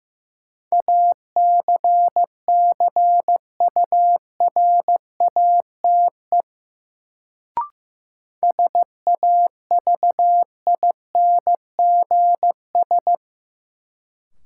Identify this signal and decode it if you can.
Morse code, heard as a keyed tone, about 15 wpm, 700 Hz. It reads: ACCURATE SAVINGS
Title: Sentences from Top 1200 Words - New Word Drill Alt - 15wpm